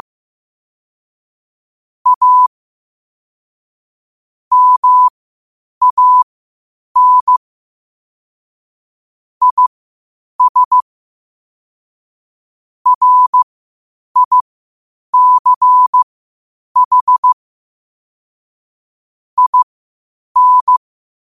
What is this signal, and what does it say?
Morse code, heard as a keyed tone, about 15 wpm, spaced as for 8 wpm, 1000 Hz. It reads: A MAN IS RICH IN
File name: qotdfiles / 8wpm